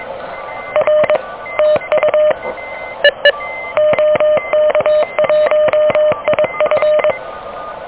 Morse code, heard as a keyed tone, about 23 wpm, 610 Hz, with noise. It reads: R TU EE OK1IF
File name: OK QRP 2023 / OK QRP 2023